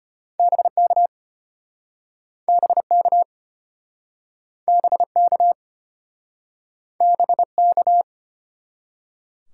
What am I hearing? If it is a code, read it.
Morse code, heard as a keyed tone, about 35 wpm, 700 Hz. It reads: BK BK BK BK